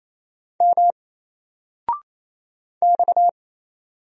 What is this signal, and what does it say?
Morse code, heard as a keyed tone, about 27 wpm, 700 Hz. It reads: M X
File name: Single-Letter-Number-28wpm